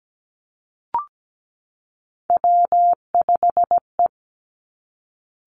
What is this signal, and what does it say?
Morse code, heard as a keyed tone, about 17 wpm, 700 Hz. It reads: W5E
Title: Three Challenging Letter-Number Combinations 17wpm